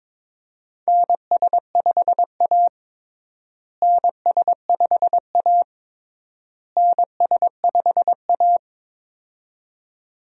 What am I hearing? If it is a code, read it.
Morse code, heard as a keyed tone, about 22 wpm, 700 Hz. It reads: NS5A NS5A NS5A
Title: USA Call Signs - 4 Characters - Repeated 3x - 22wpm